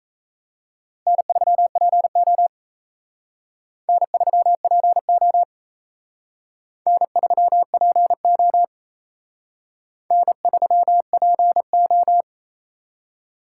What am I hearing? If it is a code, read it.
Morse code, heard as a keyed tone, about 36 wpm, 700 Hz. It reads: N3PO N3PO N3PO N3PO